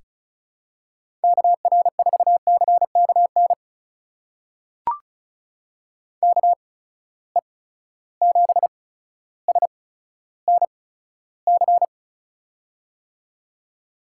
Morse code, heard as a keyed tone, about 36 wpm, spaced as for 15 wpm, 700 Hz. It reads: KR4CKN KE7SNC
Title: USA Call Signs - 8x Character Spacing - 35wpm